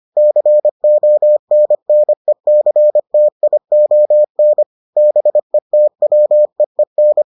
Morse code, heard as a keyed tone, about 25 wpm, 600 Hz. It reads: CONNECTION BETWEEN